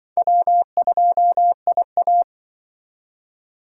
Morse code, heard as a keyed tone, about 25 wpm, 700 Hz. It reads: W2IA
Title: Speed Racing - USA 1x2 Call Signs 20wpm